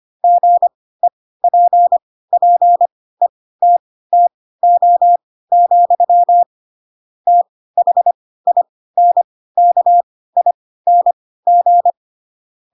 Morse code, heard as a keyed tone, about 25 wpm, spaced as for 16 wpm, 700 Hz. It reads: GEPPETTO, THINKING
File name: Pinocchio / Pinocchio ch7 10 25 ik1hge